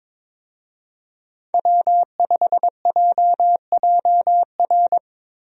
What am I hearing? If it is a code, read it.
Morse code, heard as a keyed tone, about 22 wpm, 700 Hz. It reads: W5JJR
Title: USA Call Signs - 5 Characters - Repeated 3x - 22wpm